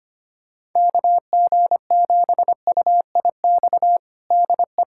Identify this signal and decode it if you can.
Morse code, heard as a keyed tone, about 25 wpm, 700 Hz. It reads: KG7UIX DE
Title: USA Callsign DE Callsign - 6 Characters - 25wpm